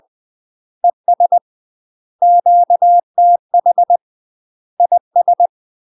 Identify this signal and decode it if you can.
Morse code, heard as a keyed tone, about 20 wpm, 700 Hz. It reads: ES QTH IS
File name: UK MASTER QSOs TRAINING / 20WPM / UK MASTER QSOs 0025